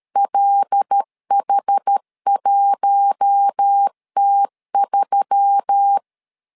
Morse code, heard as a keyed tone, about 13 wpm, 800 Hz. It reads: LH1T3
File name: LEZIONI / 5 LBVC / 5 L 12wpm 1913